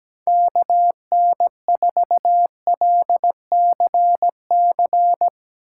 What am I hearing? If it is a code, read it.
Morse code, heard as a keyed tone, about 17 wpm, 700 Hz. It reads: KN4LCC
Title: USA Call Signs - 6 Characters - 2x Character Spacing - 17wpm